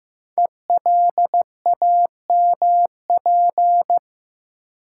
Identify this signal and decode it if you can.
Morse code, heard as a keyed tone, about 15 wpm, 700 Hz. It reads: ELAMP
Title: Code Groups - 5 Letters - 15wpm